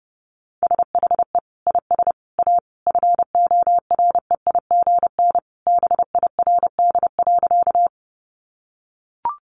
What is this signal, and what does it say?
Morse code, heard as a keyed tone, about 29 wpm, 700 Hz. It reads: SHE IS A FOREIGN BIRD.
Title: Sentences from Top 1400 Words 30wpm